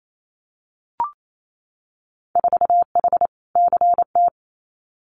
Morse code, heard as a keyed tone, about 28 wpm, 700 Hz. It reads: 4H CT
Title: Winter Field Day Exchanges Encore 28wpm